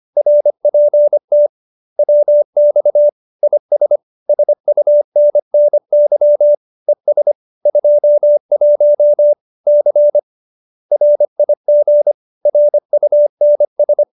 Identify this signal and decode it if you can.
Morse code, heard as a keyed tone, about 25 wpm, 590 Hz. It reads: RPT WX IS SUNNY ES 21 C RIG RUNS
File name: QSO315 25